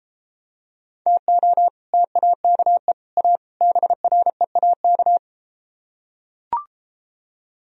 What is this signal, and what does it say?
Morse code, heard as a keyed tone, about 33 wpm, 700 Hz. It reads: TO TAKE A BREAK